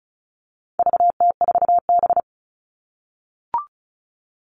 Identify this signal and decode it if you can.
Morse code, heard as a keyed tone, about 34 wpm, 700 Hz. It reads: VT4B
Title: All Cut Numbers - 4-Digit 35wpm